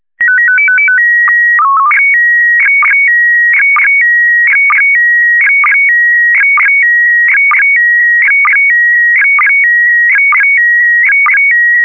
SSTV (slow-scan television), PD 290.